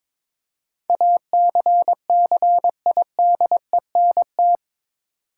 Morse code, heard as a keyed tone, about 22 wpm, 700 Hz. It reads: ACCIDENT
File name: practice / 22wpm / Eight Letter Words - 22wpm